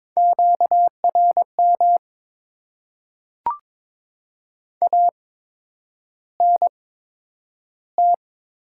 Morse code, heard as a keyed tone, about 22 wpm, spaced as for 9 wpm, 690 Hz. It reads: QRM ANT
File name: Top 90 QSO Elements - 8x Character Spacing - 22wpm